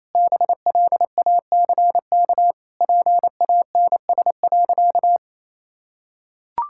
Morse code, heard as a keyed tone, about 27 wpm, 700 Hz. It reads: BLACK PANS.
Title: Pangram Challenge v2 - 28wpm